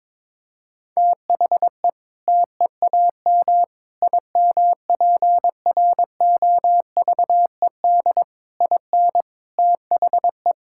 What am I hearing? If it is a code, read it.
Morse code, heard as a keyed tone, about 22 wpm, 700 Hz. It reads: THE TEAM IMPROVED IN THE
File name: Sentences with 7-Word Phrases - 22wpm